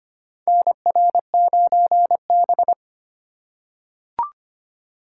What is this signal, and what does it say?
Morse code, heard as a keyed tone, about 25 wpm, 700 Hz. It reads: NR9B
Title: USA Call Signs - 3x Character Spacing - 25wpm